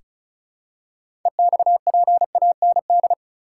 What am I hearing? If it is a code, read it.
Morse code, heard as a keyed tone, about 36 wpm, 700 Hz. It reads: EXPAND
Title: Six Letter Words - 35wpm